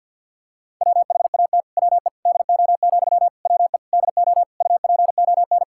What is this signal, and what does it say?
Morse code, heard as a keyed tone, about 47 wpm, 700 Hz. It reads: WHAT WE DO, WE DO UPON
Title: Sentences from Top 1600 Words 50wpm